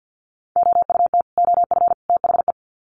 Morse code, heard as a keyed tone, about 50 wpm, 700 Hz. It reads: OUT OF THE
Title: Sets of 3 Words 50wpm